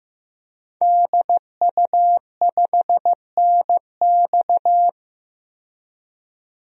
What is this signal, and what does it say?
Morse code, heard as a keyed tone, about 16 wpm, 700 Hz. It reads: DU5NX
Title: DXCC-100 Call Signs - 5 Characters - 15wpm